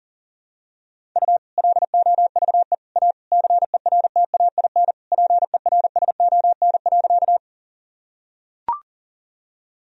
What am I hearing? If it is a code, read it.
Morse code, heard as a keyed tone, about 40 wpm, 700 Hz. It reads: U LOVE A CERTAIN PERSON.
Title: Sentences from Top 500 Words and Abbreviations 40wpm